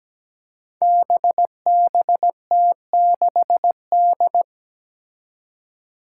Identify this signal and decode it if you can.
Morse code, heard as a keyed tone, about 17 wpm, 700 Hz. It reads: BBT6D
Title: Five Challenging Letter-Number Combinations 17wpm